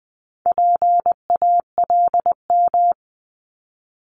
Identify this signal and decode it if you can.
Morse code, heard as a keyed tone, about 20 wpm, 700 Hz. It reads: PALM
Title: Multi-letter Phonemes 20wpm